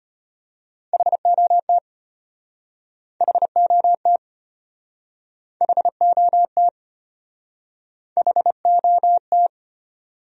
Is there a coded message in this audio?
Yes: Morse code from 0.9 to 9.5 s, heard as a keyed tone, about 34 wpm, 700 Hz. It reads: HOT HOT HOT HOT